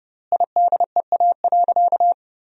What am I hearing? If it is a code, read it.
Morse code, heard as a keyed tone, about 30 wpm, 700 Hz. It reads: IDEA.
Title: 4x Word-Spacing - Sentences from Top 300 Words - 30wpm